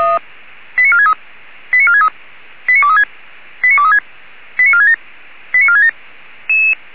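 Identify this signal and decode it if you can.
Five-tone selective call (ZVEI), heard as quick runs of five tones, tones of about 80 ms. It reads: 89463 89463 89236 89236 89467 89467
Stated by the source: ZVEI five-tone selective call received on SDR, logged by feuerwehr-orlamuende fire brigade server.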